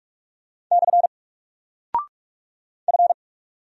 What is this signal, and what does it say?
Morse code, heard as a keyed tone, about 44 wpm, 700 Hz. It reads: / F
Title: Single Character (Letter:Number:Punctuation:Prosign) Alt 45wpm